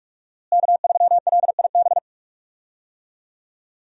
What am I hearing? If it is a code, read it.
Morse code, heard as a keyed tone, about 41 wpm, 700 Hz. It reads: K3LIB
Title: USA Call Signs - 5 Characters - Repeated 3x - 45wpm